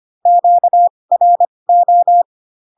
Morse code, heard as a keyed tone, about 25 wpm, spaced as for 20 wpm, 700 Hz. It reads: QRO
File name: cw / Q-codes most common repeated 3 times 15 25wpm ik1hge